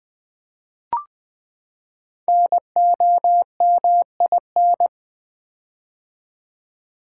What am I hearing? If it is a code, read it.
Morse code, heard as a keyed tone, about 20 wpm, 700 Hz. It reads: NOMIN
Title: Build A Word - 5 Letters At A Time - 10-Letter Words - 20wpm